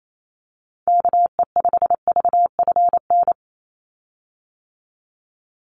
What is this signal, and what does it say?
Morse code, heard as a keyed tone, about 28 wpm, 700 Hz. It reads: KE5VFN
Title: USA Call Signs 28wpm